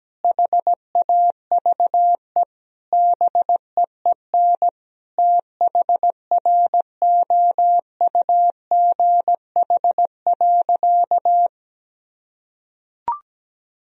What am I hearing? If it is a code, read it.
Morse code, heard as a keyed tone, about 17 wpm, 700 Hz. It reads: HAVE BEEN THROUGH.
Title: Sentences from Top 1000 Words - Review Part 1 - 17wpm